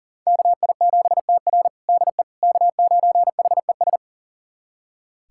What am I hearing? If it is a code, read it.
Morse code, heard as a keyed tone, about 40 wpm, 700 Hz. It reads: KI7TR DE K9HES
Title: USA Callsign DE Callsign - 5 Characters - 40wpm